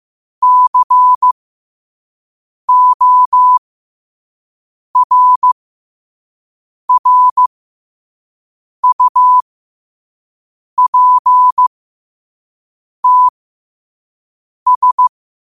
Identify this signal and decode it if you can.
Morse code, heard as a keyed tone, about 15 wpm, spaced as for 5 wpm, 1000 Hz. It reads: CORRUPTS
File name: qotdfiles / 5wpm